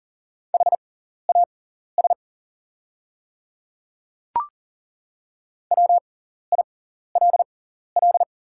Morse code, heard as a keyed tone, about 37 wpm, spaced as for 14 wpm, 690 Hz. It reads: HAS WILL
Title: 6x Character Spacing - Top 100 Words - 40wpm